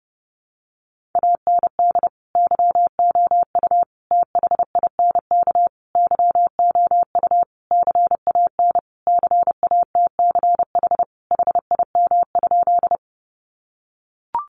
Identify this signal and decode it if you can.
Morse code, heard as a keyed tone, about 30 wpm, 700 Hz. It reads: AND YOU THINK YOU CAN CATCH HIM?
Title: Sentences from Top 1500 Words 30wpm